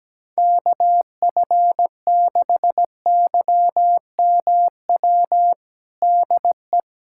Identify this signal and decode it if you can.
Morse code, heard as a keyed tone, about 17 wpm, 700 Hz. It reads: KF6YMW DE
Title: USA Callsign DE Callsign 17wpm